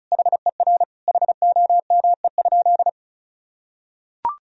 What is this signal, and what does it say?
Morse code, heard as a keyed tone, about 34 wpm, 700 Hz. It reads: HER HOME?
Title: Sentences from Top 1000 Words - Review Part 4 - 35wpm